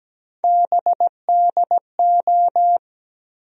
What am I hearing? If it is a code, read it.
Morse code, heard as a keyed tone, about 17 wpm, 700 Hz. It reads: BDO
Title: USA Call Sign Postfixes - 3 Letters 17wpm